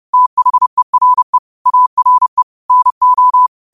Morse code, heard as a keyed tone, about 30 wpm, 1000 Hz. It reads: THERE ARE NO
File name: 2025-03-20-30wpm